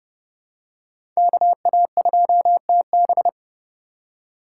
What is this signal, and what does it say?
Morse code, heard as a keyed tone, about 30 wpm, 700 Hz. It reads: KA2TB
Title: Code Groups - 5 Letter:Numbers - 30wpm